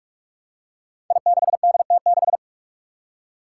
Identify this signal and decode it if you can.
Morse code, heard as a keyed tone, about 42 wpm, 700 Hz. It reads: I6DT6
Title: Five Challenging Letter-Number Combinations 45wpm